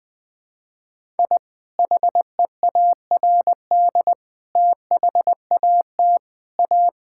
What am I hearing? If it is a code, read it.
Morse code, heard as a keyed tone, about 20 wpm, 700 Hz. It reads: I HEARD THAT A
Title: Sentences with 2-Word Phrases - Spoken Beginning - 20wpm